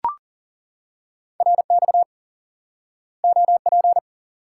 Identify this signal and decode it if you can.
Morse code, heard as a keyed tone, about 40 wpm, 700 Hz. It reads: RX OP
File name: IQE Course 3 - Lesson 89 - YAGI - Two Elements - 40wpm